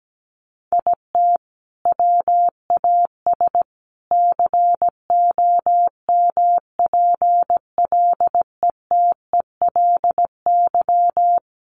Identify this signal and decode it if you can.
Morse code, heard as a keyed tone, about 17 wpm, 700 Hz. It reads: IT WAS COMPLETELY
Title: Sentences from Top 1100 Words 17wpm